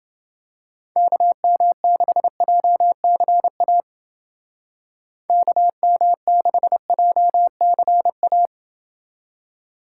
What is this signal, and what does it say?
Morse code, heard as a keyed tone, about 29 wpm, 700 Hz. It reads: KM6JCA KM6JCA